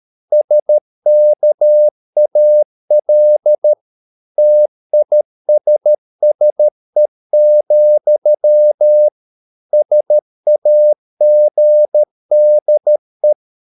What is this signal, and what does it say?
Morse code, heard as a keyed tone, about 13 wpm, 600 Hz. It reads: SKAL TISSE, SAGDE
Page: Ferie 13 wpm